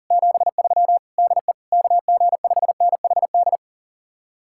Morse code, heard as a keyed tone, about 51 wpm, 700 Hz. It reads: 73 DE KG5NHD